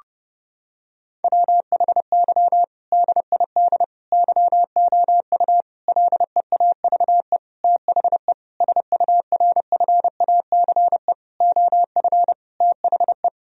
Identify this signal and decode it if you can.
Morse code, heard as a keyed tone, about 30 wpm, 700 Hz. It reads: WHY DID YOU LEAVE THE SURFACE OF THE
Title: Sentences from Top 1000 Words - Review Part 3 - 30wpm